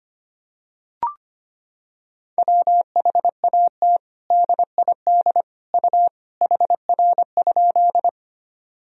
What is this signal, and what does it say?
Morse code, heard as a keyed tone, about 25 wpm, 700 Hz. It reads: WHAT DID U HR?
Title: Sentences from Top 300 Words and Abbreviations 25wpm